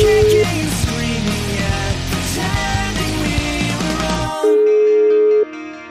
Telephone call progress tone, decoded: ringback tone

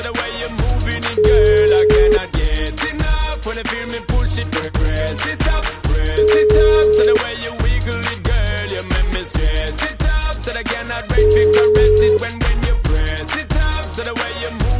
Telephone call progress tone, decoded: ringback tone

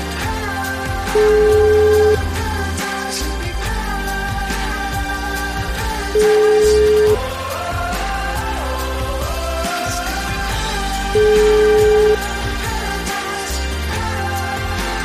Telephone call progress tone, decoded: ringback tone

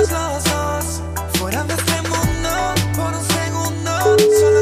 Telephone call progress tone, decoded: ringback tone

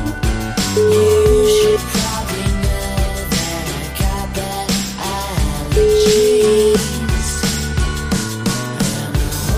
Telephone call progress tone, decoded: ringback tone

